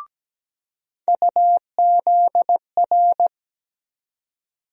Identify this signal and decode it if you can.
Morse code, heard as a keyed tone, about 17 wpm, 700 Hz. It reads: UZR
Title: USA Call Sign Postfixes - 3 Letters 17wpm